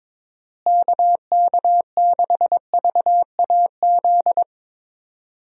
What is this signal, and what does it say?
Morse code, heard as a keyed tone, about 22 wpm, 700 Hz. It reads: KK6VAZ
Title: USA Call Signs - 6 Characters - Repeated 3x - 22wpm